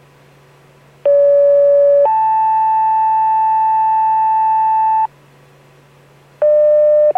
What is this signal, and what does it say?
Two-tone page, heard as a one-second tone, then a three-second tone, about 570 then 880 Hz. It is heard from 1.0 to 5.1 s.